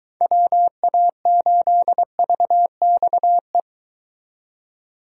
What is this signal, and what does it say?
Morse code, heard as a keyed tone, about 23 wpm, 700 Hz. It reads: WA8VXE